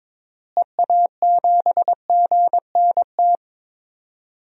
Morse code, heard as a keyed tone, about 22 wpm, 700 Hz. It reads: EA7GNT
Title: DXCC-100 Call Signs - 6 Characters - Repeated 3x - 22wpm